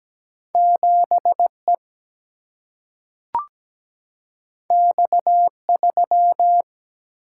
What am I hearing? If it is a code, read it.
Morse code, heard as a keyed tone, about 17 wpm, 700 Hz. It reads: 7E X3
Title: Single-Letter Single-Number 17wpm